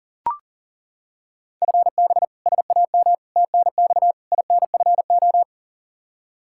Morse code, heard as a keyed tone, about 37 wpm, 690 Hz. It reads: FB SAM TNX INFO
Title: Practice QSO - Exchange 3 - More Info - 40wpm